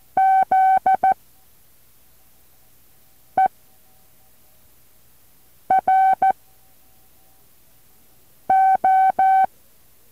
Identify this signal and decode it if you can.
Morse code, heard as a keyed tone, about 14 wpm, 760 Hz. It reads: Z E R O
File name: intl morse code2